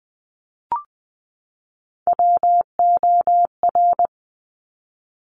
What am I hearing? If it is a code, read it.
Morse code, heard as a keyed tone, about 20 wpm, 700 Hz. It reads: WOR
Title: Top Three Letter Combinations in Words 20wpm